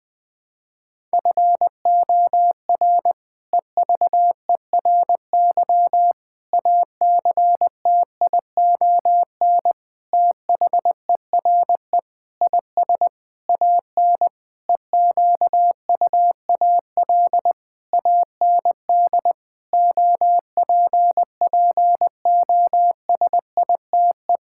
Morse code, heard as a keyed tone, about 20 wpm, 700 Hz. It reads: FOR EVERY ACTION THERE IS AN EQUAL AND OPPOSITE
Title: Sentences with 8-Word Phrases - 20wpm